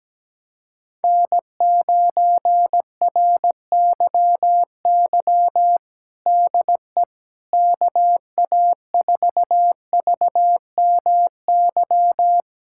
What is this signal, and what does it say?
Morse code, heard as a keyed tone, about 17 wpm, 700 Hz. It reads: N9RYY DE KA4VMY
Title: USA Callsign DE Callsign 17wpm